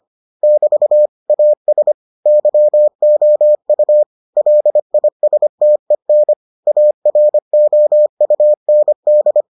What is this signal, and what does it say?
Morse code, heard as a keyed tone, about 25 wpm, 600 Hz. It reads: <BT> AS YOU LISTEN AROUND